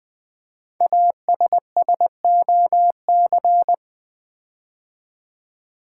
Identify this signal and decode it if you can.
Morse code, heard as a keyed tone, about 20 wpm, 700 Hz. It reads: ASSOC
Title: Build A Word - 5 Letters At A Time - 10-Letter Words - 20wpm